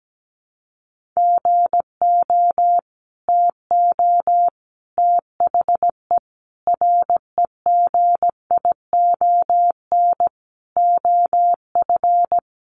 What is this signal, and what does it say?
Morse code, heard as a keyed tone, about 17 wpm, 700 Hz. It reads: GO TO THE REGION OF
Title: Sentences from Top 1500 Words 17wpm